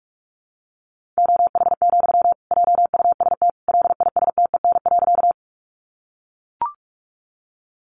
Morse code, heard as a keyed tone, about 44 wpm, 700 Hz. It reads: OH, JUST LISTEN.
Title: Sentences from Top 500 Words 45wpm